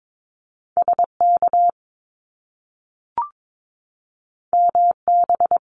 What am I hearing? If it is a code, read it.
Morse code, heard as a keyed tone, about 22 wpm, 700 Hz. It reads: SK MB
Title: Top Two Letter Combinations in Words 22wpm